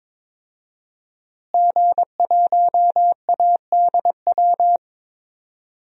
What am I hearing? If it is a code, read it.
Morse code, heard as a keyed tone, about 22 wpm, 700 Hz. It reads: G1ADW
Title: DXCC-100 Call Signs - 5 Characters - Repeated 2x - 22wpm